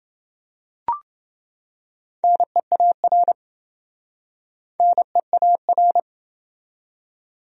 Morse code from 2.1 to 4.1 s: NEAR

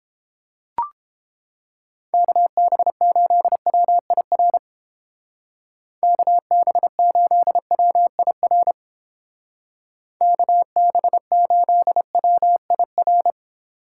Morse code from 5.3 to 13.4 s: KB8WIR KB8WIR

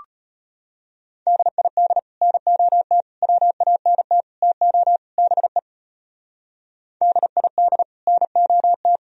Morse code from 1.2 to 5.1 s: DID NOT WANT TO

